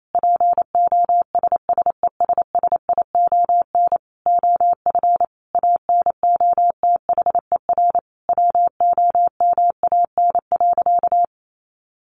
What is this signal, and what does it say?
Morse code, heard as a keyed tone, about 28 wpm, 700 Hz. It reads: POSSESSION OF ANOTHER WOMAN.
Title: Sentences from Top 1400 Words 28wpm